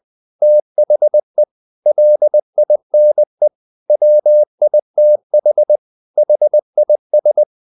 Morse code, heard as a keyed tone, about 20 wpm, 600 Hz. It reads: THE LINE WITH HIS
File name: the old man and the sea / the old man and the sea part 1 20wpm